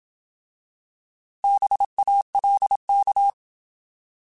Morse code, heard as a keyed tone, about 27 wpm, 800 Hz. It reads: BALK
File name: Alphabetical / Alphabetical mp3s / Bs @25